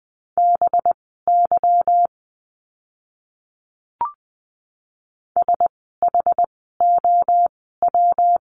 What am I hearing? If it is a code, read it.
Morse code, heard as a keyed tone, about 20 wpm, spaced as for 14 wpm, 700 Hz. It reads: BY SHOW
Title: 192 Lesson 31f - Letter v - Words 20wpm